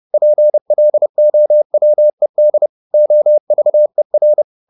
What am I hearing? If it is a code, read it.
Morse code, heard as a keyed tone, about 30 wpm, 600 Hz. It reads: PLOWED OVER